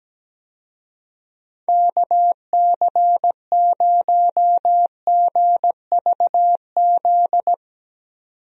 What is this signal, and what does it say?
Morse code, heard as a keyed tone, about 17 wpm, 700 Hz. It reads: KC0GVZ
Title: USA Call Signs - 6 Characters - 17wpm